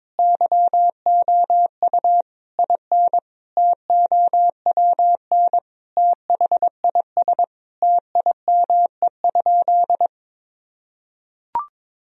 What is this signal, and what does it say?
Morse code, heard as a keyed tone, about 22 wpm, 700 Hz. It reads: YOU IN TOWN THIS TIME?
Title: Sentences from Top 1000 Words - Review Part 2 - 22wpm